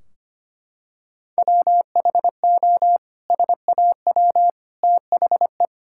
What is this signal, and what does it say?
Morse code, heard as a keyed tone, about 25 wpm, 700 Hz. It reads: WHO SAW THE
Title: Sentences from Top 1600 Words 25wpm